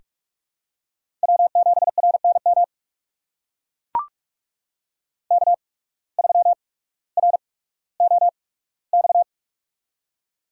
Morse code, heard as a keyed tone, about 44 wpm, spaced as for 19 wpm, 700 Hz. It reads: W7RNM K3ROX
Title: USA Call Signs - 5 Characters - 8x Character Spacing - 45wpm